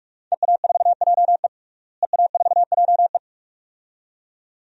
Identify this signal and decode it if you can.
Morse code, heard as a keyed tone, about 44 wpm, 700 Hz. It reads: EA4JE EA4JE
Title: DXCC-100 Call Signs - 5 Characters - Repeated 2x - 45wpm